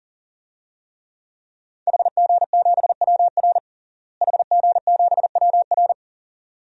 Morse code, heard as a keyed tone, about 37 wpm, 700 Hz. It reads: HG7WR HG7WR
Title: DXCC-100 Call Signs - 5 Characters - Repeated 2x - 40wpm